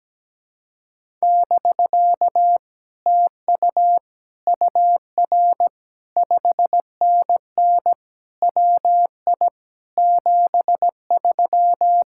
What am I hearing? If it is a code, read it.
Morse code, heard as a keyed tone, about 17 wpm, 700 Hz. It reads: <BK> TU UR 5NN WI 73
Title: Speed-Racing Common QSO Phrases - 17wpm